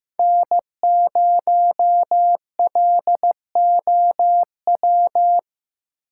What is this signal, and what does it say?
Morse code, heard as a keyed tone, about 15 wpm, 700 Hz. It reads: N0LOW
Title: USA Call Signs - 5 Characters - 2x Character Spacing - 15wpm